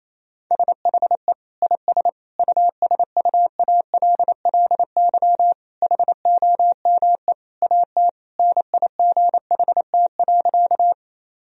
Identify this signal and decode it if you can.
Morse code, heard as a keyed tone, about 28 wpm, 700 Hz. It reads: SHE IS USUALLY HOME AT NIGHT.
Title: Sentences from Top 1000 Words - Review Part 3 - 28wpm